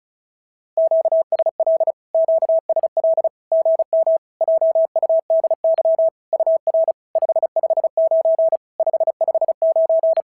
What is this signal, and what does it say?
Morse code, heard as a keyed tone, about 33 wpm, 650 Hz. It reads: QSL QSL GM JUDY UR 559 559
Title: SOTA QSO Practice 35wpm